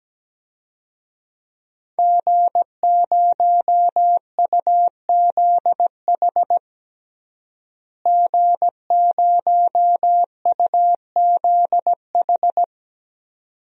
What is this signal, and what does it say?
Morse code, heard as a keyed tone, about 17 wpm, 700 Hz. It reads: G0UZH G0UZH
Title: DXCC-100 Call Signs - 5 Characters - Repeated 2x - 17wpm